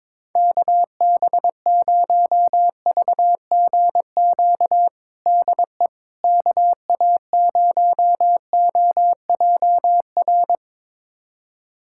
Morse code, heard as a keyed tone, about 22 wpm, 700 Hz. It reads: KB0VGQ DE KA0OJR
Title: USA Callsign DE Callsign 22wpm